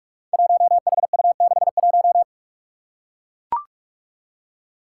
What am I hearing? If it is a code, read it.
Morse code, heard as a keyed tone, about 44 wpm, 700 Hz. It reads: 1HU61
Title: Five Challenging Letter-Number Combinations 45wpm